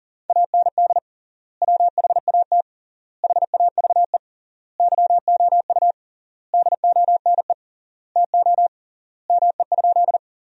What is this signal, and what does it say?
Morse code, heard as a keyed tone, about 40 wpm, 700 Hz. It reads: AND WHAT HAVE YOU DONE TO ME?
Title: 3x Word-Spacing - Sentences from Top 500 Words - 40wpm